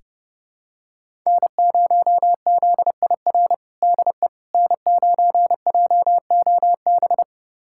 Morse code, heard as a keyed tone, about 32 wpm, 700 Hz. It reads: N0ZIR DE N9JOB
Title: USA Callsign DE Callsign - 5 Characters - 30wpm